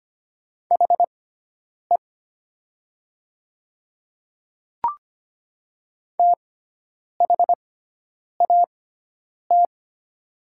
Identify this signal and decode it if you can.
Morse code, heard as a keyed tone, about 25 wpm, spaced as for 9 wpm, 700 Hz. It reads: HE THAT
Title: 6x Character Spacing - Top 100 Words - 25wpm